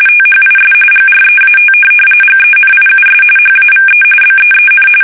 RTTY, about 45.45 baud. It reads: THIS GREAT GENIUS HAVE BEEN
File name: fsk850 / benjamin-g-lamme